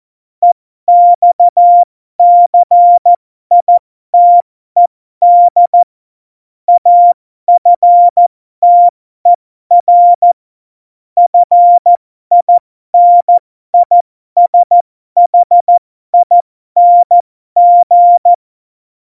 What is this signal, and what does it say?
Morse code, heard as a keyed tone, about 14 wpm, spaced as for 12 wpm, 700 Hz. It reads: EXCITED AFTER FINISHING
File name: Morsekurs / Witze